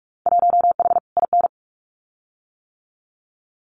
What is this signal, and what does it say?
Morse code, heard as a keyed tone, about 44 wpm, 700 Hz. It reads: 1H IN
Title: Winter Field Day Exchanges 45wpm